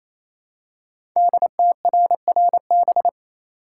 Morse code, heard as a keyed tone, about 28 wpm, 700 Hz. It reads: DTRRB
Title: Code Groups - 5 Letters - 28wpm